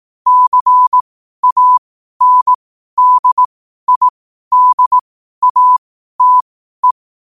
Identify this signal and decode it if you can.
Morse code, heard as a keyed tone, about 18 wpm, spaced as for 12 wpm, 1000 Hz. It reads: CANDIDATE
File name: qotdfiles / 12wpm